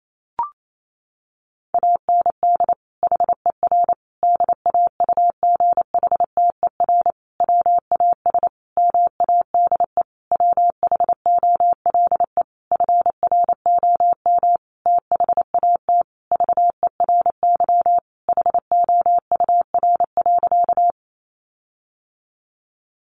Morse code, heard as a keyed tone, about 28 wpm, 700 Hz. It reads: AND HER DAUGHTER WAS MADE WHOLE FROM THAT VERY HOUR.
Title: Sentences from Top 1500 Words 28wpm